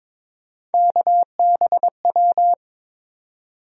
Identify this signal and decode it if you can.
Morse code, heard as a keyed tone, about 22 wpm, 700 Hz. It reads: KBW
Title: USA Call Sign Postfixes - 3 Letters 22wpm